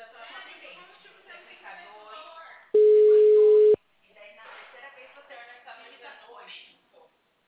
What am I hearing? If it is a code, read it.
Ringback tone (Europe, 425 Hz), with speech.
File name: out- -2014- - - .60711